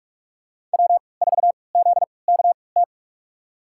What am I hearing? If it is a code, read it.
Morse code, heard as a keyed tone, about 46 wpm, spaced as for 26 wpm, 700 Hz. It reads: W4ZKT
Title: USA Call Signs - 5 Characters - 3x Character Spacing - 45wpm